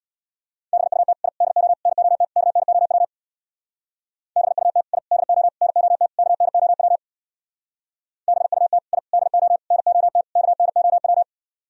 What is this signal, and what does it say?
Morse code, heard as a keyed tone, about 62 wpm, 690 Hz. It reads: BUT I DO NOT KNOW BUT I DO NOT KNOW BUT I DO NOT KNOW
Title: Speed-Racing - Sets of 5 Words 50wpm